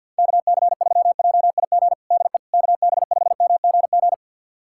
Morse code, heard as a keyed tone, about 53 wpm, 700 Hz. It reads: KC3JIG DE KB5MGG
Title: USA Callsign DE Callsign 50wpm